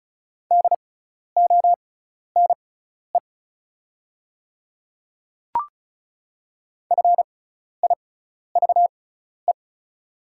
Morse code, heard as a keyed tone, about 36 wpm, spaced as for 12 wpm, 690 Hz. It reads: DONE FIVE
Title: 6x Character Spacing - Top 500 Words - 35wpm